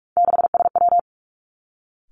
Morse code, heard as a keyed tone, about 46 wpm, 700 Hz. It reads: 6SW
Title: Three Challenging Letter-Number Combinations 45wpm